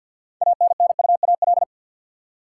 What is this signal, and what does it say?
Morse code, heard as a keyed tone, about 52 wpm, 700 Hz. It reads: ANNUAL
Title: Six Letter Words - 50wpm